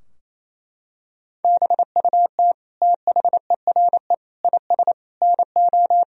Morse code, heard as a keyed tone, about 28 wpm, 700 Hz. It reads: BUT THERE IS NO
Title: Sentences from Top 1000 Words - Review Part 3 - 28wpm